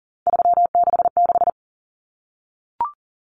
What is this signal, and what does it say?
Morse code, heard as a keyed tone, about 38 wpm, 700 Hz. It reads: 366
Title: Numbers - 3-Digit 40wpm